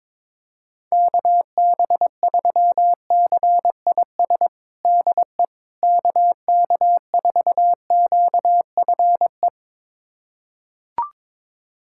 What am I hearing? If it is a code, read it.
Morse code, heard as a keyed tone, about 21 wpm, 700 Hz. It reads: KB3CIS DE KK4QFE
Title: USA Callsign DE Callsign - 6 Characters - 22wpm